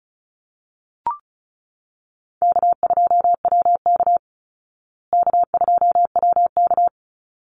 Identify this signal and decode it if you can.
Morse code, heard as a keyed tone, about 34 wpm, 700 Hz. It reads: K2WK K2WK
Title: Easy Call Signs Repeated 2x - 35wpm